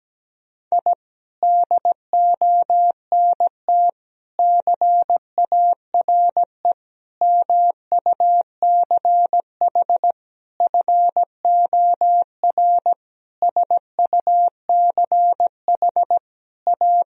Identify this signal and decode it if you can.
Morse code, heard as a keyed tone, about 17 wpm, 700 Hz. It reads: I DONT CARE MUCH FOR SUCH A
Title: Sentences from Top 1000 Words - Review Part 4 - 17wpm